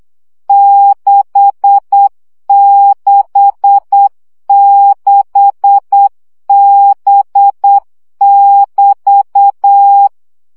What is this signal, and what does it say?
Morse code, heard as a keyed tone, about 8 wpm, 800 Hz. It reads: 666B<BT>
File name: CW gyakorlas b 6 egyenlo minusz 7 baud